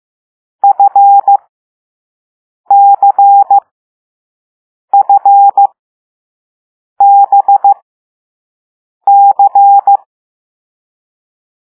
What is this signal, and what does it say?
Morse code, heard as a keyed tone, about 15 wpm, spaced as for 5 wpm, 800 Hz. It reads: FCFBC
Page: Play Lesson 6 C - all letters in previous lessons